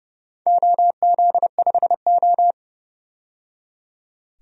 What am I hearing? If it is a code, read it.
Morse code, heard as a keyed tone, about 30 wpm, 700 Hz. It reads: OZ5O
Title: DXCC-100 Call Signs - 4 Characters - 30wpm